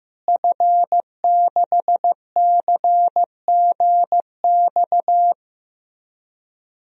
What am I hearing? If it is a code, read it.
Morse code, heard as a keyed tone, about 16 wpm, 700 Hz. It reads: F6CGX
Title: DXCC-100 Call Signs - 5 Characters - Repeated 3x - 15wpm